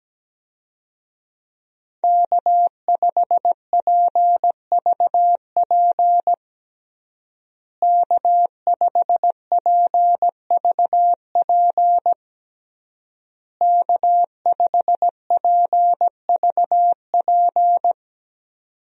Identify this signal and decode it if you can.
Morse code, heard as a keyed tone, about 17 wpm, 700 Hz. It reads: K5PVP K5PVP K5PVP
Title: USA Call Signs - 5 Characters - Repeated 3x - 17wpm